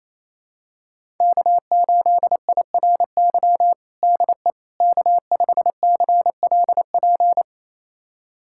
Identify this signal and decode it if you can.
Morse code, heard as a keyed tone, about 28 wpm, 700 Hz. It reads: K8IRY DE K5CLP
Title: USA Callsign DE Callsign - 5 Characters - 28wpm